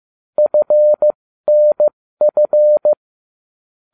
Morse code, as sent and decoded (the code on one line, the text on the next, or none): ..-. -. ..-.
FNF